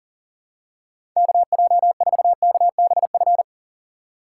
Morse code, heard as a keyed tone, about 40 wpm, 700 Hz. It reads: KJ4KBF
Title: USA Call Signs - 6 Characters - 4x Character Spacing - 40wpm